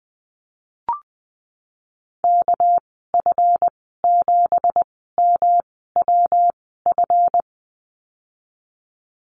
Morse code, heard as a keyed tone, about 20 wpm, spaced as for 15 wpm, 700 Hz. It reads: KF7MWF
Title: 200 Lesson 32g - Letter k - Call Signs 20wpm